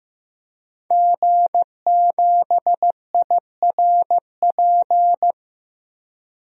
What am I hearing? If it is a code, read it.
Morse code, heard as a keyed tone, about 15 wpm, 700 Hz. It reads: G7IRP
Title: DXCC-100 Call Signs - 5 Characters - Repeated 3x - 15wpm